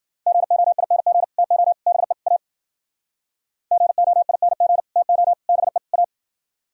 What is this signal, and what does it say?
Morse code, heard as a keyed tone, about 53 wpm, 700 Hz. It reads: GOING TO BE A GOING TO BE A